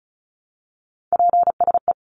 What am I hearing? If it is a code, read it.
Morse code, heard as a keyed tone, about 34 wpm, 700 Hz. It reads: PSE
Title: Abbreviations 35wpm